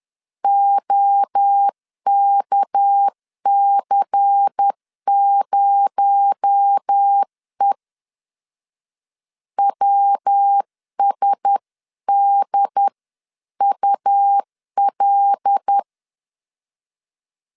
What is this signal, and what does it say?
Morse code, heard as a keyed tone, about 10 wpm, 800 Hz. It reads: OKC0E WSDUL
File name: CW / DET / 1 6 1500C S C 10wpm 4252